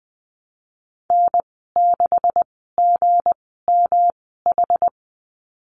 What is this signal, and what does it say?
Morse code, heard as a keyed tone, about 20 wpm, spaced as for 15 wpm, 700 Hz. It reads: N6GMH
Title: 238 Lesson 38g - Letter q - Call Signs 20wpm